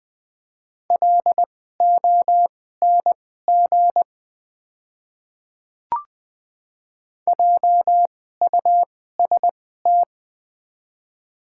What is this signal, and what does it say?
Morse code, heard as a keyed tone, about 20 wpm, spaced as for 15 wpm, 700 Hz. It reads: LONG JUST
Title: 2x Character Spacing - Top 200 Words - 20wpm